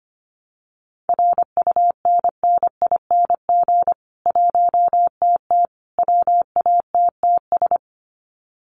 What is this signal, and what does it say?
Morse code, heard as a keyed tone, about 25 wpm, 700 Hz. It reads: RUNNING 1TT WATTS
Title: Common QSO Phrases 25wpm